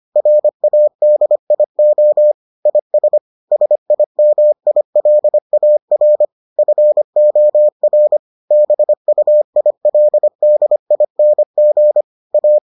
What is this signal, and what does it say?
Morse code, heard as a keyed tone, about 25 wpm, 600 Hz. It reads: RADIO IS SIMILAR FOR BUILDING A